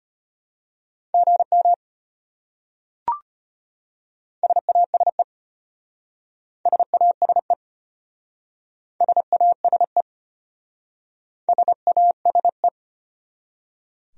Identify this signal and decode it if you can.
Morse code, heard as a keyed tone, about 33 wpm, 700 Hz. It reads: GM SASE SASE SASE SASE